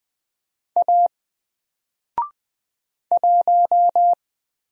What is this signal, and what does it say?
Morse code, heard as a keyed tone, about 20 wpm, 700 Hz. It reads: A 1
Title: Single Letter-Number 20wpm